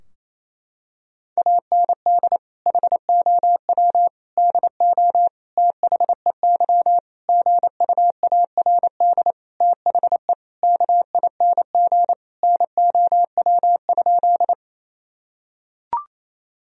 Morse code, heard as a keyed tone, about 28 wpm, 700 Hz. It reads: AND HOW DO THEY GUARD THE KING NOW?
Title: Sentences from Top 1000 Words - Review Part 4 - 28wpm